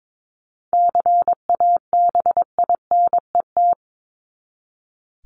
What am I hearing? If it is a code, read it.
Morse code, heard as a keyed tone, about 22 wpm, 700 Hz. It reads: CABINET
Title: Top 1751-2000 Words 22wpm